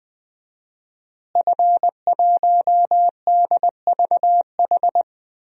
Morse code, heard as a keyed tone, about 20 wpm, 700 Hz. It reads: F1DVH
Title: DXCC-100 Call Signs - 5 Characters - Repeated 2x - 20wpm